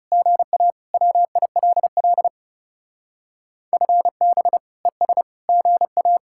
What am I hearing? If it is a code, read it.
Morse code, heard as a keyed tone, about 36 wpm, 700 Hz. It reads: GA WILL FB ES GA
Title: Speed-Racing Common QSO Phrases - 30wpm